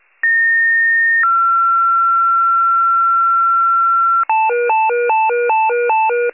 Two-tone page, heard as a one-second tone, then a three-second tone, about 1850 then 1360 Hz. It is heard from 0.2 to 4.2 s.